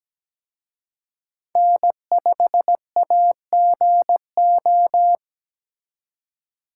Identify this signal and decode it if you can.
Morse code, heard as a keyed tone, about 22 wpm, 700 Hz. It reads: N5AGO